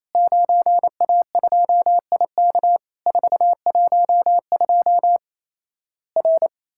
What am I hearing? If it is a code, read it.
Morse code, heard as a keyed tone, about 28 wpm, 700 Hz. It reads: 9A2IK 412 R
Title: Worked All Europe DX Contest - QTC Exchange - 28wpm